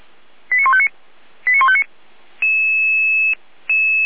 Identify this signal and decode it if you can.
Five-tone selective call (ZVEI), heard as quick runs of five tones, tones of about 70 ms. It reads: 89159 89159
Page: /455/SDR/ZVEI/